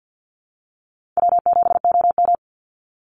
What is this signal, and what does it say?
Morse code, heard as a keyed tone, about 50 wpm, 700 Hz. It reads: W7OM
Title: Call Signs Repeated 3x - 50wpm